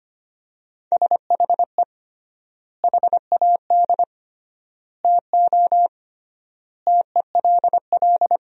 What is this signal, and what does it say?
Morse code, heard as a keyed tone, about 25 wpm, 700 Hz. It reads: SHE HAD TO TELL
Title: 3x Word-Spacing - Sentences from Top 200 Words - 25wpm